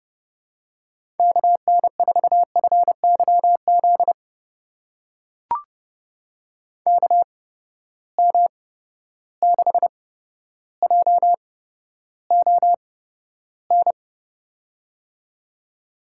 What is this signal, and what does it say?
Morse code, heard as a keyed tone, about 30 wpm, spaced as for 13 wpm, 700 Hz. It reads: KN4FYZ KM6JON
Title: USA Call Signs - 6 Characters - 8x Character Spacing - 30wpm